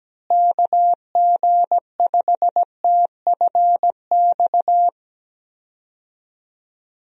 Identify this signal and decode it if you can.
Morse code, heard as a keyed tone, about 17 wpm, 700 Hz. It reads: KG5TFX
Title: USA Call Signs - 6 Characters - Repeated 2x - 17wpm